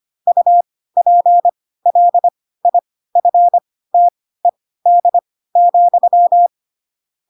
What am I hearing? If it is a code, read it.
Morse code, heard as a keyed tone, about 25 wpm, spaced as for 16 wpm, 700 Hz. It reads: UPLIFTED,
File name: Pinocchio / Pinocchio ch1 10 25 ik1hge